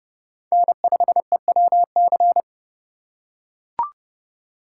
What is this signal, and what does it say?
Morse code, heard as a keyed tone, about 29 wpm, 700 Hz. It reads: N5EWC